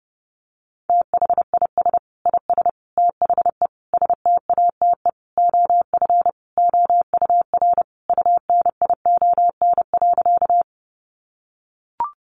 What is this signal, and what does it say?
Morse code, heard as a keyed tone, about 30 wpm, 700 Hz. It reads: THIS IS THE STATE OF OUR UNION.
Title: Sentences from Top 1400 Words 30wpm